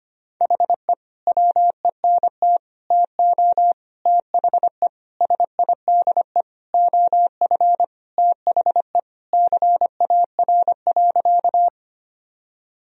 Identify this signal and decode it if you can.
Morse code, heard as a keyed tone, about 25 wpm, 700 Hz. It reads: HE WENT TO THE SIDE OF THE CAR.
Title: Sentences from Top 1000 Words - Review Part 1 - 25wpm